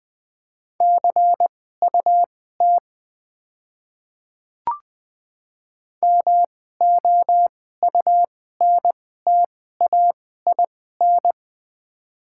Morse code, heard as a keyed tone, about 20 wpm, spaced as for 15 wpm, 690 Hz. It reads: CUT MOUNTAIN
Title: 2x Character Spacing - Top 300 Words - 20wpm